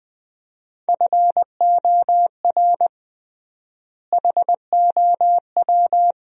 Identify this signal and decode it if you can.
Morse code, heard as a keyed tone, about 20 wpm, 700 Hz. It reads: FOR HOW
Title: 3x Word-Spacing - Sentences from Top 200 Words - 20wpm